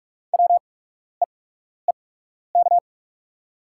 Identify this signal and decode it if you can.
Morse code, heard as a keyed tone, about 44 wpm, spaced as for 12 wpm, 690 Hz. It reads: WEEK
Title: 8x Character Spacing - Top 500 Words - 45wpm